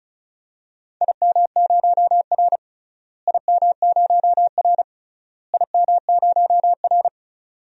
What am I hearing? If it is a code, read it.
Morse code, heard as a keyed tone, about 34 wpm, 700 Hz. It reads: IM0R IM0R IM0R
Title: DXCC-100 Call Signs - 4 Characters - Repeated 3x - 35wpm